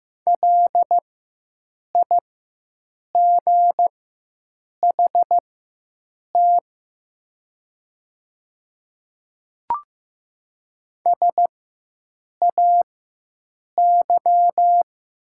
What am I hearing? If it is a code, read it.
Morse code, heard as a keyed tone, about 15 wpm, spaced as for 7 wpm, 700 Hz. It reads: LIGHT SAY
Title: 4x Character Spacing - Top 300 Words - 15wpm